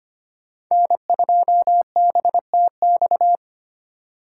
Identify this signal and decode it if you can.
Morse code, heard as a keyed tone, about 25 wpm, 700 Hz. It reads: N2BTX
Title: K1USN SST QSO Practice - 25wpm